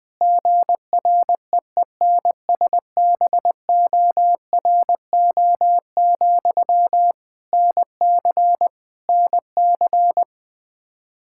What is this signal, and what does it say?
Morse code, heard as a keyed tone, about 20 wpm, 700 Hz. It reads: GREENSBORO, NC NC
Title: Practice QSO - Exchange 1 - Beginning - 20wpm